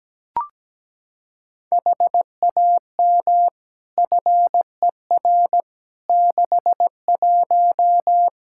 Morse code, heard as a keyed tone, about 17 wpm, 700 Hz. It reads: HAM FER 61
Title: K1USN SST QSO Practice 17wpm